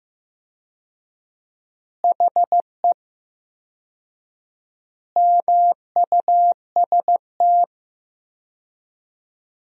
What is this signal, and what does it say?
Morse code, heard as a keyed tone, about 15 wpm, 700 Hz. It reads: HE MUST
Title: 4x Word-Spacing - Sentences from Top 500 Words - 15wpm